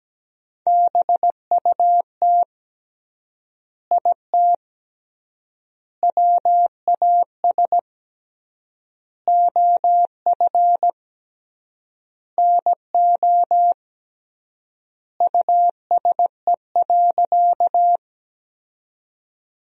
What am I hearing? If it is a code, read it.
Morse code, heard as a keyed tone, about 17 wpm, 700 Hz. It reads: BUT IT WAS OF NO USE.
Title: 3x Word-Spacing - Sentences from Top 100 Words - 17wpm